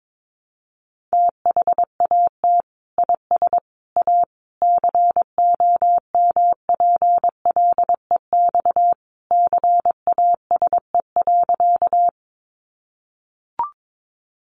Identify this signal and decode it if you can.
Morse code, heard as a keyed tone, about 22 wpm, 700 Hz. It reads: THAT IS A COMPLEX CASE.
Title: Sentences from Top 900 Words 22wpm